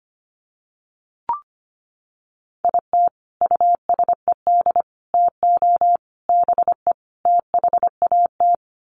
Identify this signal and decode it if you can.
Morse code, heard as a keyed tone, about 25 wpm, 700 Hz. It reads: IT USED TO BE THAT
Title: Sets of 5 Words Encore2 25wpm